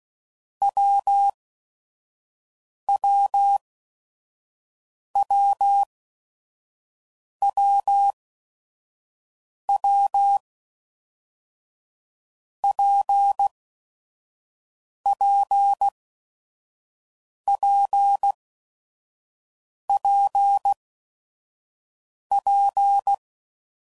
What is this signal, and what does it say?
Morse code, heard as a keyed tone, about 16 wpm, spaced as for 5 wpm, 800 Hz. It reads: WWWWW PPPPP
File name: Combinations / Difficult Combos / WPJ WPJ1